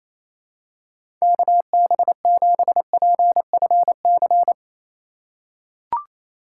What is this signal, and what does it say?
Morse code, heard as a keyed tone, about 28 wpm, 700 Hz. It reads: KB7PFC
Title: USA Call Signs - 4x Character Spacing - 28wpm